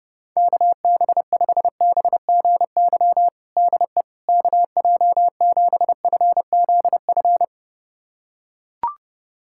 Morse code, heard as a keyed tone, about 32 wpm, 700 Hz. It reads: KB5BGY DE KJ7FZF
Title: USA Callsign DE Callsign 30wpm